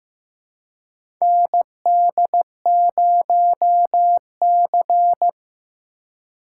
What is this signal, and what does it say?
Morse code, heard as a keyed tone, about 15 wpm, 700 Hz. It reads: ND0C
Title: USA Call Signs - 4 Characters - 4x Character Spacing - 15wpm